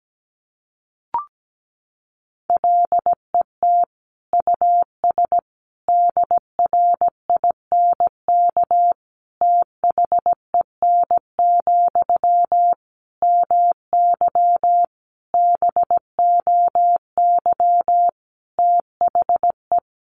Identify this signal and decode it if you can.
Morse code, heard as a keyed tone, about 17 wpm, 700 Hz. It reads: LET US DRINK THEN, MY BOY THE
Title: Sentences from Top 1500 Words 17wpm